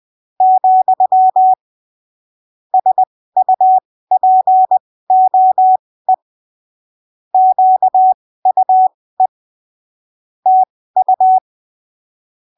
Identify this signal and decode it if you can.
Morse code, heard as a keyed tone, about 20 wpm, spaced as for 15 wpm, 750 Hz. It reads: , SUPOE QUE TU
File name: arquivos / texto / texto 26 20ppm